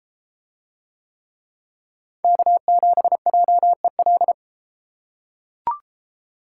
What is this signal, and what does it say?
Morse code, heard as a keyed tone, about 33 wpm, 700 Hz. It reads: K7JEL